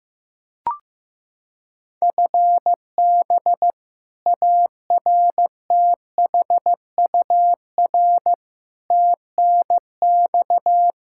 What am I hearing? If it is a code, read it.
Morse code, heard as a keyed tone, about 15 wpm, 700 Hz. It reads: FB ARTHUR TNX
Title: Practice QSO - Exchange 3 - More Info - 15wpm